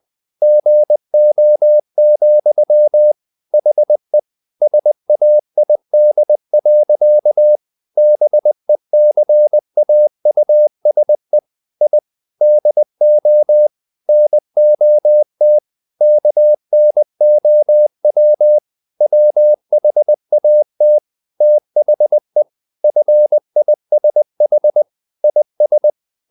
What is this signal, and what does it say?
Morse code, heard as a keyed tone, about 20 wpm, 600 Hz. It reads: GO, HE SAID. BECAUSE I DO NOT KNOW WHAT THE FISH IS
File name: the old man and the sea / the old man and the sea part 1 20wpm